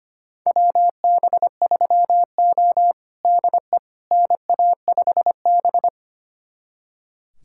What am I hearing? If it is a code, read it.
Morse code, heard as a keyed tone, about 25 wpm, 700 Hz. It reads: WB3O DE NA5B
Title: USA Callsign DE Callsign - 4 Characters - 25wpm